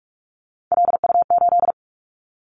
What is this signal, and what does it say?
Morse code, heard as a keyed tone, about 46 wpm, 700 Hz. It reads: LU8
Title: Callsign Prefixes for Top 100 DXCC - 45wpm